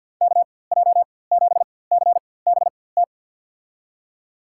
Morse code, heard as a keyed tone, about 53 wpm, spaced as for 24 wpm, 700 Hz. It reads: KJ7CBT
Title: USA Call Signs - 6 Characters - 4x Character Spacing - 50wpm